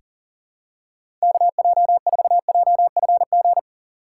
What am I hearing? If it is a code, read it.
Morse code, heard as a keyed tone, about 40 wpm, 700 Hz. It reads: KJ4JFG
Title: USA Call Signs - 2x Character Spacing - 40wpm